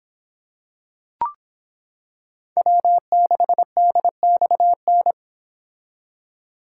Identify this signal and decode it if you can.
Morse code, heard as a keyed tone, about 26 wpm, 700 Hz. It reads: W6DXN